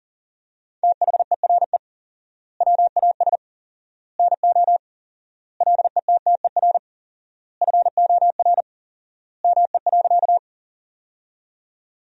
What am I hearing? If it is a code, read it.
Morse code, heard as a keyed tone, about 37 wpm, 700 Hz. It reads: THERE WAS NO LETTER FOR ME.
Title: 4x Word-Spacing - Sentences from Top 300 Words - 40wpm